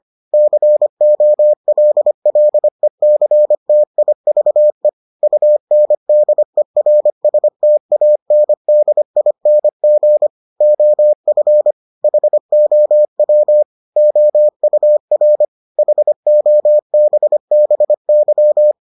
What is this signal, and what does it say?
Morse code, heard as a keyed tone, about 25 wpm, 600 Hz. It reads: COLLECTIVE UNDERSTANDING OF HOW OUR HOBBY